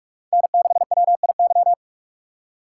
Morse code, heard as a keyed tone, about 42 wpm, 700 Hz. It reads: N6WIY
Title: USA Call Signs - 5 Characters - 4x Character Spacing - 45wpm